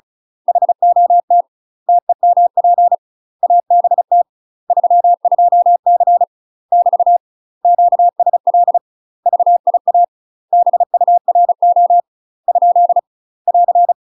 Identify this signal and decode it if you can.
Morse code, heard as a keyed tone, about 34 wpm, 700 Hz. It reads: HOT TEMP ABT 32C <BT> QSL VIA BURO ? <AR>
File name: UK MASTER QSOs TRAINING / 35WPM / UK MASTER QSOs 0029